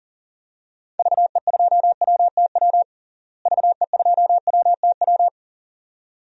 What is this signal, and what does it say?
Morse code, heard as a keyed tone, about 40 wpm, 700 Hz. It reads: VE2WTW VE2WTW
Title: DXCC-100 Call Signs - 6 Characters - Repeated 2x - 40wpm